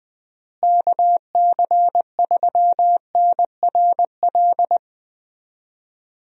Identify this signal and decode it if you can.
Morse code, heard as a keyed tone, about 20 wpm, 700 Hz. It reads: KC3NRL
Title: USA Call Signs - 6 Characters - Repeated 3x - 20wpm